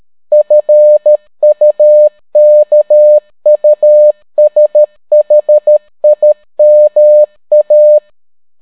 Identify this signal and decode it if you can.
Morse code, heard as a keyed tone, about 13 wpm, 600 Hz. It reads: FUKUSHIMA